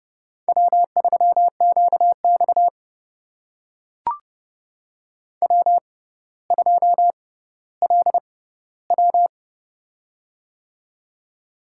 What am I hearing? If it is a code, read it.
Morse code, heard as a keyed tone, about 31 wpm, spaced as for 15 wpm, 700 Hz. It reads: W3QX W2LW
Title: USA Call Signs - 4 Characters - 6x Character Spacing - 30wpm